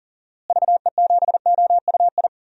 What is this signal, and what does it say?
Morse code, heard as a keyed tone, about 40 wpm, 700 Hz. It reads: VE7OUI
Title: DXCC-100 Call Signs - 6 Characters - Repeated 3x - 40wpm